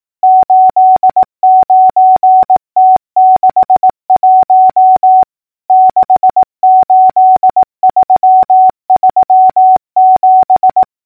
Morse code, heard as a keyed tone, about 18 wpm, 750 Hz. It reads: 89T61 68337
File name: numbers / V2-morse-clicks